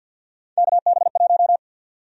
Morse code, heard as a keyed tone, about 47 wpm, 700 Hz. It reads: KB1
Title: USA Call Sign Prefixes - Repeated 2x - 50wpm